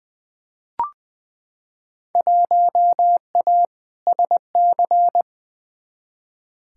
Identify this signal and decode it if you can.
Morse code, heard as a keyed tone, about 20 wpm, 700 Hz. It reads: 1A SC
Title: Field Day Exchanges - 20wpm